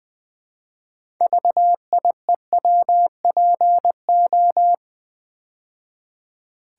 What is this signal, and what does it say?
Morse code, heard as a keyed tone, about 20 wpm, 700 Hz. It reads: VIEWPO
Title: Build A Word - 3 Letters At A Time - 9-Letter Words - 20wpm